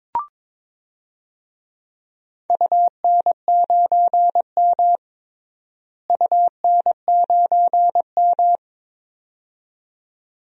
Morse code, heard as a keyed tone, about 21 wpm, 700 Hz. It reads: UN9M UN9M
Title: DXCC-100 Call Signs - 4 Characters - Repeated 2x - 22wpm